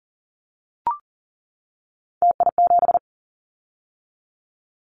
Morse code, heard as a keyed tone, about 40 wpm, 700 Hz. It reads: TI7
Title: Callsign Prefixes for Top 100 DXCC - 40wpm